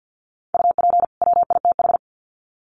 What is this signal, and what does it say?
Morse code, heard as a keyed tone, about 50 wpm, 700 Hz. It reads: UP WITH
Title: Sets of 2 Words 50wpm